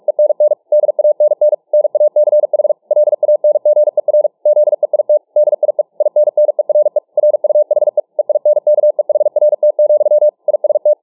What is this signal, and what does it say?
Morse code, heard as a keyed tone, about 47 wpm, 600 Hz. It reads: ENN DANN NACH LANGER ZEIT DIE INNERE RUHE EINKEHRT, IST